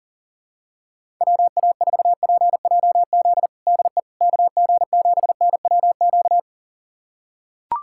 Morse code, heard as a keyed tone, about 40 wpm, 700 Hz. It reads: WA4PJZ DE KG7NWQ
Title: USA Callsign DE Callsign 40wpm